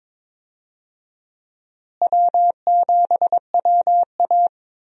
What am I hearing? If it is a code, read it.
Morse code, heard as a keyed tone, about 22 wpm, 700 Hz. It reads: W7WA
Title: USA Call Signs - 4 Characters - Repeated 3x - 22wpm